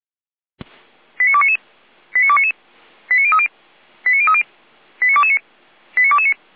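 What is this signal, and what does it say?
Five-tone selective call (ZVEI), heard as quick runs of five tones, tones of about 70 ms. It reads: 89299 89299 89030 89030 89229 89229
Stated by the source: ZVEI five-tone call received by SDR, from the feuerwehr-orlamuende.org fire brigade audio server.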